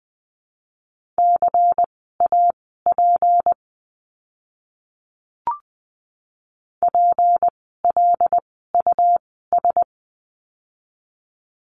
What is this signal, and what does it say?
Morse code, heard as a keyed tone, about 20 wpm, spaced as for 14 wpm, 700 Hz. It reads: CAP PLUS
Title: 153 Lesson 25f - Letter p - Words 20wpm